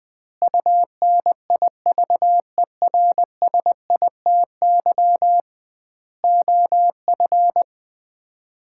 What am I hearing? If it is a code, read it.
Morse code, heard as a keyed tone, about 20 wpm, 700 Hz. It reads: UNIVERSITY OF
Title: 2x Word-Spacing - Sets of 4 Words - 20wpm